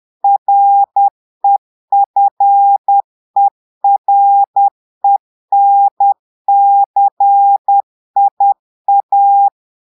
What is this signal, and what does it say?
Morse code, heard as a keyed tone, about 10 wpm, 800 Hz. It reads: REFERENCIA
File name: Portugais - texte en clair 1 - vitesse - 10 mots minute